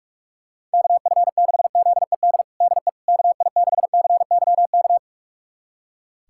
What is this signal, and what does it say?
Morse code, heard as a keyed tone, about 41 wpm, 700 Hz. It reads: KF6ZED DE KI6CYK
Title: USA Callsign DE Callsign - 6 Characters - 45wpm